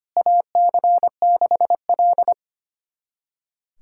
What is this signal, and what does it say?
Morse code, heard as a keyed tone, about 25 wpm, 700 Hz. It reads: AC6L
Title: USA Callsign DE Callsign - 4 Characters - 25wpm